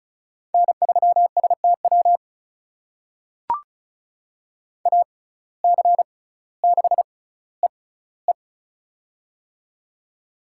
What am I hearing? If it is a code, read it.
Morse code, heard as a keyed tone, about 36 wpm, spaced as for 18 wpm, 700 Hz. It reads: N3STW AC6EE
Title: USA Call Signs - 5 Characters - 6x Character Spacing - 35wpm